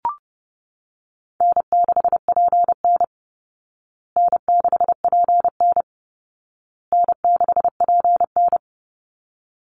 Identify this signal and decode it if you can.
Morse code, heard as a keyed tone, about 28 wpm, 700 Hz. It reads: N6PN N6PN N6PN
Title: Easy Call Signs Repeated 3x - 30wpm